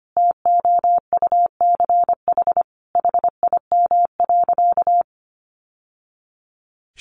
Morse code, heard as a keyed tone, about 25 wpm, 700 Hz. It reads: TOUCH HIM.
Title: Sentences from Top 1200 Words 25wpm